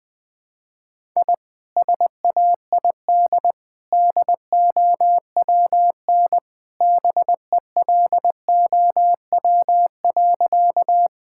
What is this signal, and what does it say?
Morse code, heard as a keyed tone, about 20 wpm, 700 Hz. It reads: I SAID DOWN BELOW.
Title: Sentences from Top 1000 Words - Review Part 5 - 20wpm